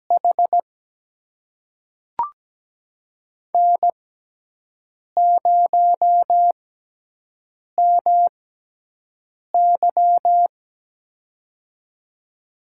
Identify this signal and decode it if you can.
Morse code, heard as a keyed tone, about 17 wpm, spaced as for 6 wpm, 700 Hz. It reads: H N0MY
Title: USA Call Signs - 4 Characters - 6x Character Spacing - 17wpm